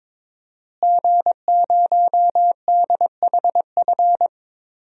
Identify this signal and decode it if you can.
Morse code, heard as a keyed tone, about 22 wpm, 700 Hz. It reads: G0DHF
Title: DXCC-100 Call Signs - 5 Characters - Repeated 2x - 22wpm